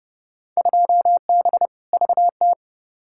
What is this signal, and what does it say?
Morse code, heard as a keyed tone, about 32 wpm, 700 Hz. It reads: 2B VT
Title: Field Day Exchanges Encore - 30wpm